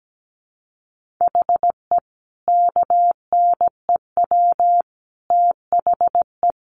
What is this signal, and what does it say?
Morse code, heard as a keyed tone, about 17 wpm, 700 Hz. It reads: HE KNEW THE
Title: Sentences from Top 1400 Words 17wpm